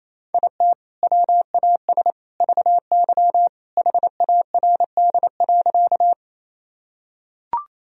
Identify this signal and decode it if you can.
Morse code, heard as a keyed tone, about 28 wpm, 700 Hz. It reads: IT WAS VY HARD.
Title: Sentences from Top 300 Words and Abbreviations 28wpm